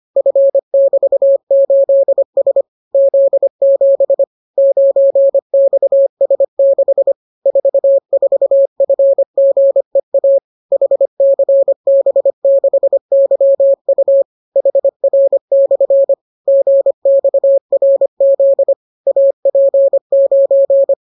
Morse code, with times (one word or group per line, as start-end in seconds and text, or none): 0.2-2.6 F<BT>8S
2.9-4.2 Z7
4.6-7.1 9XS6
7.5-10.4 44FGEA
10.7-14.2 HCB6YU
14.6-16.1 HR/
16.5-18.7 GXRZ
19.1-20.9 AP9